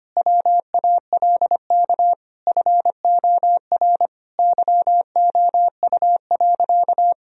Morse code, heard as a keyed tone, about 25 wpm, 700 Hz. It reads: WALK FOR YOU.
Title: Sentences from Top 1000 Words - Review Part 1 - 25wpm